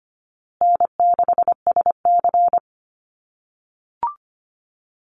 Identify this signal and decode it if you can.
Morse code, heard as a keyed tone, about 25 wpm, 700 Hz. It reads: N6SC
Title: USA Call Signs Repeated 3x 25wpm